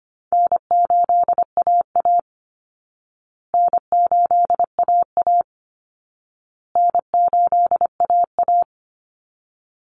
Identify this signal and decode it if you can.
Morse code, heard as a keyed tone, about 25 wpm, 700 Hz. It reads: N8AA N8AA N8AA
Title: Call Signs Repeated 2x - 25wpm